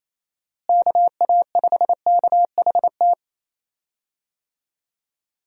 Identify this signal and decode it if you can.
Morse code, heard as a keyed tone, about 27 wpm, 700 Hz. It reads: KA5KHT
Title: USA Call Signs - 6 Characters - Repeated 3x - 28wpm